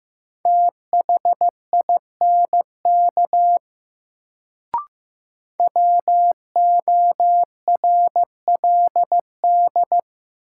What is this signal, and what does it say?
Morse code, heard as a keyed tone, about 15 wpm, 700 Hz. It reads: THINK WORLD
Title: Sets of 2 Words - New Word Drill - 15wpm